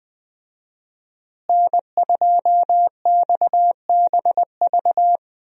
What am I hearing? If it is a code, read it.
Morse code, heard as a keyed tone, about 20 wpm, 700 Hz. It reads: N2XBV
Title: USA Call Signs - 5 Characters - Repeated 3x - 20wpm